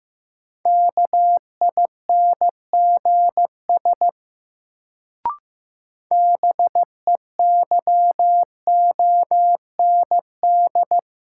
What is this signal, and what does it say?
Morse code, heard as a keyed tone, about 15 wpm, 700 Hz. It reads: KINGS BEYOND
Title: Sentences from Top 1000 Words - Review Part 6 - New Word Drill - 15wpm